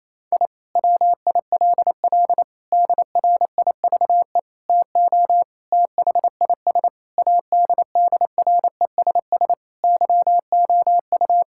Morse code, heard as a keyed tone, about 27 wpm, 700 Hz. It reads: I WILL DRIVE TO THIS ADDRESS YOU
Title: Sentences from Top 1000 Words - Review Part 6 - 28wpm